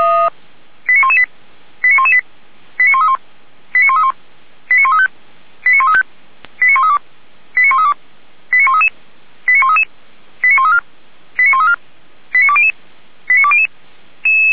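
Five-tone selective call (ZVEI), heard as quick runs of five tones, tones of about 70 ms. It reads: 89118 89118 89131 89131 89135 89135 89132 89132 89133 89133 89134 89134 89200 89200
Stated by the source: Five-tone ZVEI selective call received by SDR.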